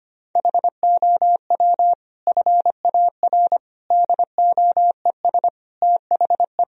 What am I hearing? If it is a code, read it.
Morse code, heard as a keyed tone, about 25 wpm, 700 Hz. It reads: HOW FAR DOES THE
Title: Sentences from Top 1000 Words - Review Part 4 - 25wpm